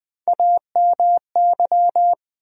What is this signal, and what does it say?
Morse code, heard as a keyed tone, about 21 wpm, 700 Hz. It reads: AMY
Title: USA Common Names - Max 4 Letters - 20wpm